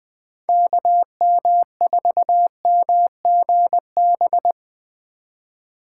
Morse code, heard as a keyed tone, about 19 wpm, 700 Hz. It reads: KM4MGB
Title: Speed Racing - USA 2x3 Call Signs 17wpm